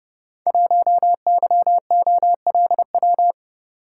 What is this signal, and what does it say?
Morse code, heard as a keyed tone, about 30 wpm, 700 Hz. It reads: 1YOLW
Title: Code Groups - 5 Letter:Numbers - 30wpm